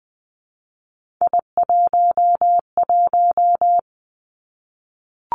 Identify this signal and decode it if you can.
Morse code, heard as a keyed tone, about 20 wpm, 700 Hz. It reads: I11
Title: Three Challenging Letter-Number Combinations 20wpm